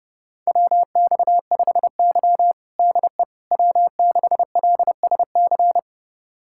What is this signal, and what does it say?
Morse code, heard as a keyed tone, about 30 wpm, 700 Hz. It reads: WX5Y DE W6LSC
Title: USA Callsign DE Callsign 30wpm